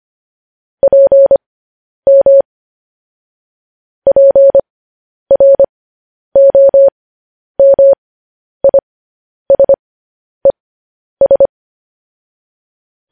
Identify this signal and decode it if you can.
Morse code, heard as a keyed tone, about 25 wpm, spaced as for 10 wpm, 550 Hz. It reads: PM PROMISES
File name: podcast / News headlines-1x-25x10